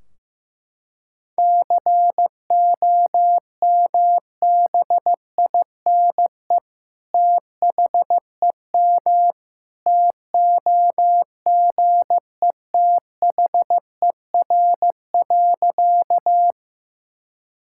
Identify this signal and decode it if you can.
Morse code, heard as a keyed tone, about 15 wpm, 700 Hz. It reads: COMBINE THEM TOGETHER.
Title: Sentences from Top 1600 Words 15wpm